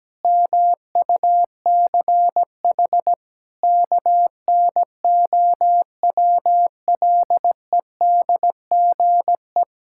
Morse code, heard as a keyed tone, about 17 wpm, 700 Hz. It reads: MUCH KNOWLEDGE
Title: Sentences from Top 1000 Words - Review Part 6 - 17wpm